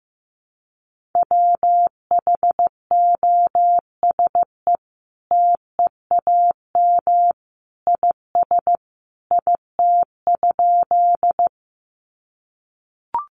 Morse code, heard as a keyed tone, about 15 wpm, 700 Hz. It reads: WHOSE TEAM IS IT?
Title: Sentences from Top 1500 Words 15wpm